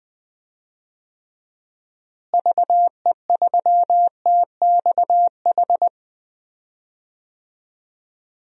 Morse code, heard as a keyed tone, about 20 wpm, 700 Hz. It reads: VE3TXH
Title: DXCC-100 Call Signs - 6 Characters - Repeated 3x - 20wpm